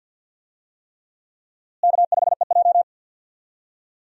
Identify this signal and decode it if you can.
Morse code, heard as a keyed tone, about 52 wpm, 700 Hz. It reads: K5EJ
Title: USA Call Signs - 4 Characters - Repeated 3x - 50wpm